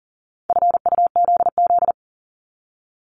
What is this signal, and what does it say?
Morse code, heard as a keyed tone, about 39 wpm, 690 Hz. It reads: FUZZ
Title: Multi-letter Phonemes 40wpm